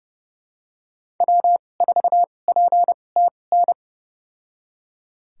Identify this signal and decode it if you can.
Morse code, heard as a keyed tone, about 30 wpm, spaced as for 22 wpm, 700 Hz. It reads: W4PTN
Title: USA Call Signs - 5 Characters - 2x Character Spacing - 30wpm